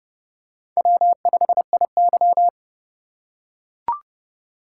Morse code, heard as a keyed tone, about 30 wpm, 700 Hz. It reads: W5IY
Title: USA Call Signs - 4 Characters - 6x Character Spacing - 30wpm